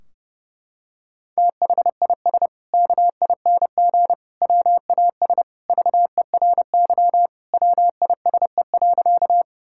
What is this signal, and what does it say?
Morse code, heard as a keyed tone, about 30 wpm, 700 Hz. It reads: THIS KING WAS VERY WISE.
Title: Sentences from Top 1600 Words 30wpm